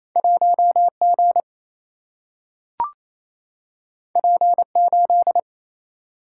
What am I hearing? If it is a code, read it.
Morse code, heard as a keyed tone, about 28 wpm, 700 Hz. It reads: 1G P8
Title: Single-Letter Single-Number 28wpm